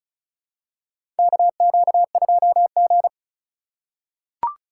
Morse code, heard as a keyed tone, about 34 wpm, 700 Hz. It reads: KQ2G
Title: USA Call Signs - 3x Character Spacing - 35wpm